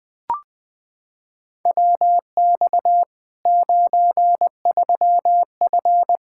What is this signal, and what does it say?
Morse code, heard as a keyed tone, about 20 wpm, 700 Hz. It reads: WX 93F
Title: Practice QSO - Exchange 2 - Info - 20wpm